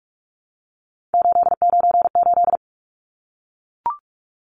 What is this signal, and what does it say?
Morse code, heard as a keyed tone, about 44 wpm, 700 Hz. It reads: 898
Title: Numbers - 3-Digit 45wpm